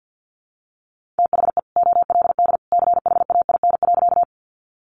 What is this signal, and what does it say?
Morse code, heard as a keyed tone, about 50 wpm, 700 Hz. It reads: THE OLD CHAIN.
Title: Sentences from Top 1500 Words 50wpm